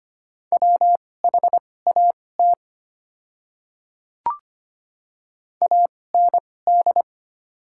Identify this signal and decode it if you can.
Morse code, heard as a keyed tone, about 25 wpm, spaced as for 18 wpm, 700 Hz. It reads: WHAT AND
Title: 2x Character Spacing - Top 100 Words - 25wpm